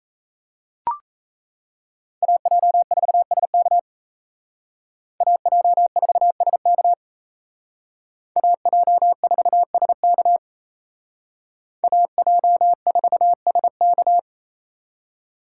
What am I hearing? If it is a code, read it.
Morse code, heard as a keyed tone, about 35 wpm, 700 Hz. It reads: AJ4SK AJ4SK AJ4SK AJ4SK